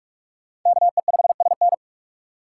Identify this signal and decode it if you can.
Morse code, heard as a keyed tone, about 58 wpm, 700 Hz. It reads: KE5SN